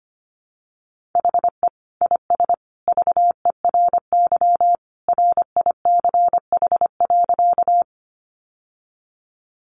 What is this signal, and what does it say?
Morse code, heard as a keyed tone, about 25 wpm, 700 Hz. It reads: HE IS VERY RICH.
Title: Sentences from Top 1100 Words 25wpm